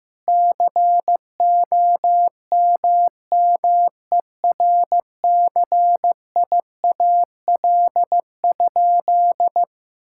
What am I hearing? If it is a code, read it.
Morse code, heard as a keyed tone, about 15 wpm, 700 Hz. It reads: COMMERCIAL?
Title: Sentences from Top 1000 Words - Review Part 5 - 15wpm